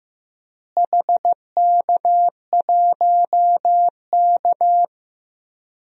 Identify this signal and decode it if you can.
Morse code, heard as a keyed tone, about 16 wpm, 700 Hz. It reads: HK1K
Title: DXCC-100 Call Signs - 4 Characters - Repeated 3x - 15wpm